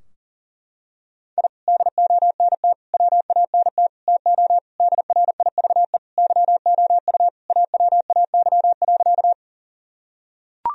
Morse code, heard as a keyed tone, about 40 wpm, 700 Hz. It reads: I DONT WANT TO DRIVE YOU AWAY.
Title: Sentences from Top 1000 Words - Review Part 2 - 40wpm